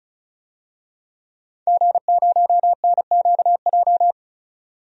Morse code, heard as a keyed tone, about 36 wpm, 700 Hz. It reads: G0NQJ
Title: DXCC-100 Call Signs - 5 Characters - 35wpm